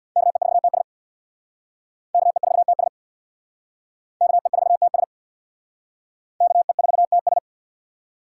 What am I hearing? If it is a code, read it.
Morse code, heard as a keyed tone, about 61 wpm, 700 Hz. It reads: KE4TS KE4TS KE4TS KE4TS
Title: Speed Racing - USA 2x2 Call Signs 50wpm